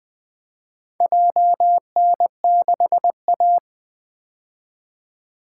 Morse code, heard as a keyed tone, about 20 wpm, 700 Hz. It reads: JN6A
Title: Four Letter-Number Combinations - 20wpm